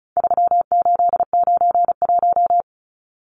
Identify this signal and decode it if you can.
Morse code, heard as a keyed tone, about 34 wpm, 700 Hz. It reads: 3891
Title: Numbers - 4-Digit 35wpm